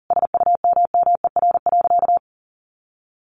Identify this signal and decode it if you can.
Morse code, heard as a keyed tone, about 37 wpm, 700 Hz. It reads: SUMMER.
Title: Sentences from Top 1500 Words 40wpm